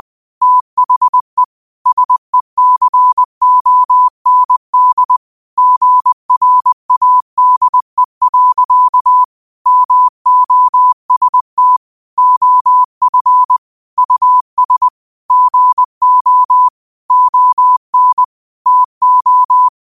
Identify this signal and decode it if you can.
Morse code, heard as a keyed tone, about 20 wpm, 1000 Hz. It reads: THE SECOND GRADE. MOST OF US GO ON TO